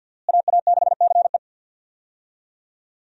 Morse code, heard as a keyed tone, about 52 wpm, 700 Hz. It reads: AA6CE
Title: USA Call Signs - 5 Characters - Repeated 3x - 50wpm